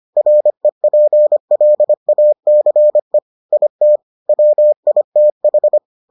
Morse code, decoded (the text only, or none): REPLACE IT WITH